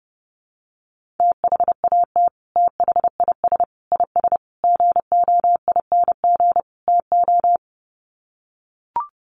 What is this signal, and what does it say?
Morse code, heard as a keyed tone, about 28 wpm, 700 Hz. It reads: THAT THIS IS GOING TO
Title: Sets of 5 Words Encore2 30wpm